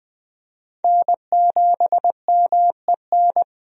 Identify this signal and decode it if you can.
Morse code, heard as a keyed tone, about 20 wpm, 700 Hz. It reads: N7MEN
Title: USA Call Signs - 5 Characters - 20wpm